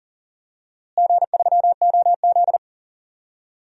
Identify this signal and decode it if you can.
Morse code, heard as a keyed tone, about 40 wpm, 700 Hz. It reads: G3OZ
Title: DXCC-100 Call Signs - 4 Characters - Repeated 2x - 40wpm